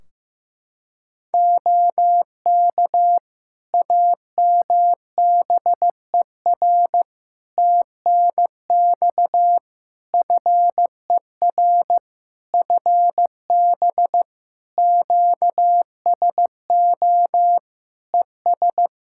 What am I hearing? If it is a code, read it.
Morse code, heard as a keyed tone, about 15 wpm, 700 Hz. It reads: OK AMBER TNX FER FB QSO ES
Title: Practice QSO - Exchange 4 - Ending - 15wpm